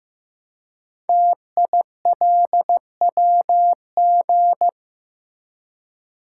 Morse code, heard as a keyed tone, about 15 wpm, 700 Hz. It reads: TILWG
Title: Code Groups - 5 Letters - 15wpm